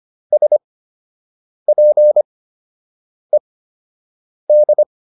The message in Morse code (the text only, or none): SPED